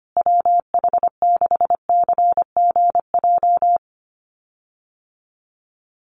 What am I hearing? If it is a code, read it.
Morse code, heard as a keyed tone, about 25 wpm, 700 Hz. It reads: WH6CGJ
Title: USA Call Signs Repeated 3x 25wpm